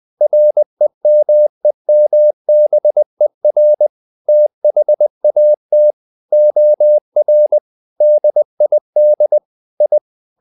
Morse code, heard as a keyed tone, about 20 wpm, 600 Hz. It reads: REMEMBER THAT OR DID I
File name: the old man and the sea / the old man and the sea part 1 20wpm